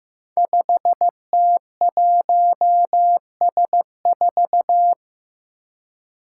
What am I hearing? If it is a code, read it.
Morse code, heard as a keyed tone, about 15 wpm, 700 Hz. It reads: 5T1S4
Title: Five Challenging Letter-Number Combinations 15wpm